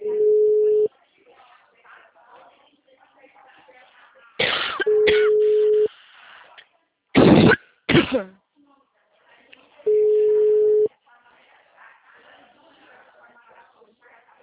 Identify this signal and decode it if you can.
Ringback tone (Europe, 425 Hz), with speech.